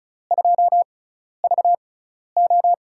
Morse code, heard as a keyed tone, about 36 wpm, spaced as for 12 wpm, 700 Hz. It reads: 2VO
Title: USA Call Signs - 5 Characters - 6x Character Spacing - 35wpm